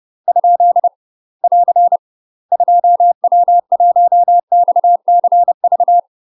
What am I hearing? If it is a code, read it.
Morse code, heard as a keyed tone, about 30 wpm, 700 Hz. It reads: ? <AR> 2W1XCV
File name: UK MASTER QSOs TRAINING / 30WPM / UK MASTER QSOs 0013